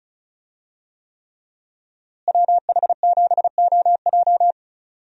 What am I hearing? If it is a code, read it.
Morse code, heard as a keyed tone, about 36 wpm, 700 Hz. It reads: WH7OJ
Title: USA Call Signs - 5 Characters - Repeated 3x - 35wpm